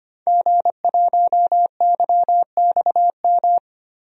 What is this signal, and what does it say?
Morse code, heard as a keyed tone, about 25 wpm, 700 Hz. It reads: G1YXM
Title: DXCC-100 Call Signs - 5 Characters - 25wpm